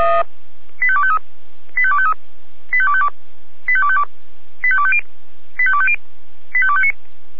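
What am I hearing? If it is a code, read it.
Five-tone selective call (ZVEI), heard as quick runs of five tones, tones of about 70 ms. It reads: 85253 85253 85252 85252 85250 85250 85259
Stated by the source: ZVEI five-tone selective call 85248 received by SDR.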